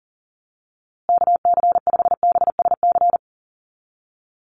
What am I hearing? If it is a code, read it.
Morse code, heard as a keyed tone, about 40 wpm, 700 Hz. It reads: KC5BSC
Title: USA Call Signs Repeated 3x 40wpm